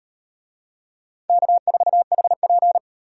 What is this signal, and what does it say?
Morse code, heard as a keyed tone, about 38 wpm, 700 Hz. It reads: K4HP